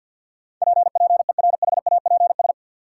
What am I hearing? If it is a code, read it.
Morse code, heard as a keyed tone, about 50 wpm, 700 Hz. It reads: PPERHAPS
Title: Sentences from Top 1100 Words - New Word Drill - 50wpm